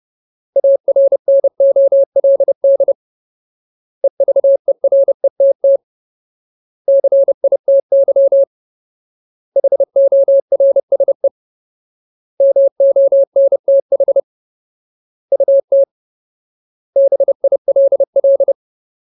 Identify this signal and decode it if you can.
Morse code, heard as a keyed tone, about 30 wpm, 550 Hz. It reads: ARNOLD EVERETT CITY HORSE MONTH UT BILL
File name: Ultimate Word List / 30WPM Ultimate